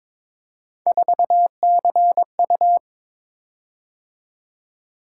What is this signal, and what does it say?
Morse code, heard as a keyed tone, about 22 wpm, 700 Hz. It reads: 4CU
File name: practice / 22wpm / Three Letter-Number Combinations - 22wpm